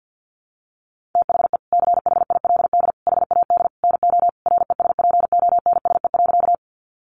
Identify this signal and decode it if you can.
Morse code, heard as a keyed tone, about 50 wpm, 700 Hz. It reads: THE CHILD HAD NO RESPONSE.
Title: Sentences from Top 1400 Words 50wpm